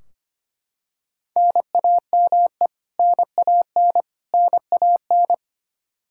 Morse code, heard as a keyed tone, about 25 wpm, 700 Hz. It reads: NAME NAN NAN
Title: Practice QSO - Exchange 1 - Beginning - 25wpm